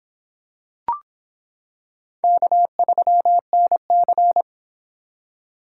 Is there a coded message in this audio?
Yes: Morse code from 2.2 to 4.4 s, heard as a keyed tone, about 26 wpm, 700 Hz. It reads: K3NC